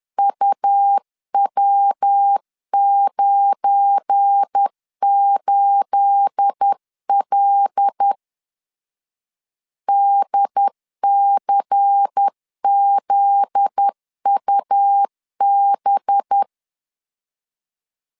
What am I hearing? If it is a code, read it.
Morse code, heard as a keyed tone, about 10 wpm, 800 Hz. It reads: UW98L DCZUB
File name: LEZIONI / 9 28 10wpm 2842